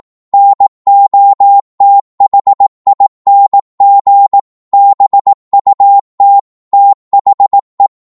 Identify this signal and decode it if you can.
Morse code, heard as a keyed tone, about 18 wpm, 800 Hz. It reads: NOTHING BUT THE
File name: 18wpm